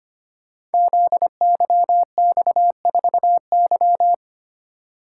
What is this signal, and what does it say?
Morse code, heard as a keyed tone, about 25 wpm, 700 Hz. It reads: ZYX4Y
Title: Code Groups - 5 Letter:Numbers - 25wpm